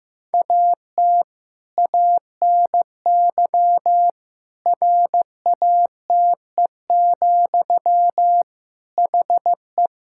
Morse code, heard as a keyed tone, about 15 wpm, 700 Hz. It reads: AT ANY RATE, HE
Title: Sentences from Top 1000 Words - Review Part 5 - 15wpm